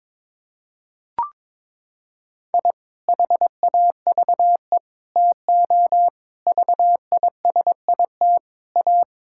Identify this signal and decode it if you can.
Morse code, heard as a keyed tone, about 21 wpm, 700 Hz. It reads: I HAVE TO VISIT A
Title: Sentences from Top 1000 Words - Review Part 6 - 22wpm